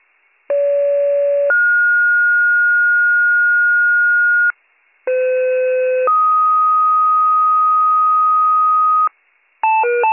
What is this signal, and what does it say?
Two-tone page, heard as a one-second tone, then a three-second tone, about 570 then 1430 Hz, about 510 then 1250 Hz, with noise. It is heard from 0.5 to 9.1 s.